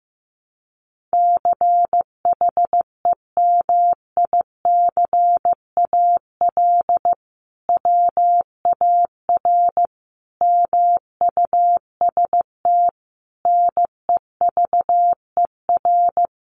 Morse code, heard as a keyed tone, about 15 wpm, 700 Hz. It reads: CHEMICAL WAR MUST NEVER
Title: Sentences from Top 1200 Words 15wpm